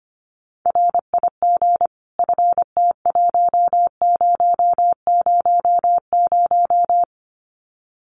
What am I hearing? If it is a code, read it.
Morse code, heard as a keyed tone, about 25 wpm, 700 Hz. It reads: RIG FT1000
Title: Common QSO Phrases 25wpm